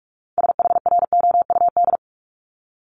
Morse code, heard as a keyed tone, about 46 wpm, 700 Hz. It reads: SHROUD
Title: Multi-letter Phonemes Repeated 3x 45wpm